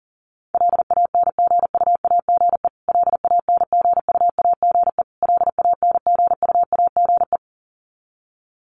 Morse code, heard as a keyed tone, about 40 wpm, 700 Hz. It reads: LANGUAGE LANGUAGE LANGUAGE
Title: Top 500 Words Repeated 40wpm